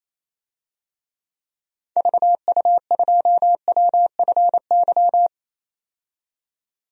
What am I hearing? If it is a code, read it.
Morse code, heard as a keyed tone, about 28 wpm, 700 Hz. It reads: VU2WFY
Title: DXCC-100 Call Signs - 6 Characters - 28wpm